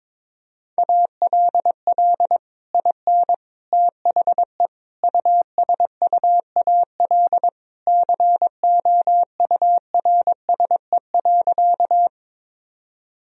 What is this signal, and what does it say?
Morse code, heard as a keyed tone, about 22 wpm, 700 Hz. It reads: ALL IN THE USUAL COURSE.
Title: Sentences from Top 1000 Words - Review Part 2 - 22wpm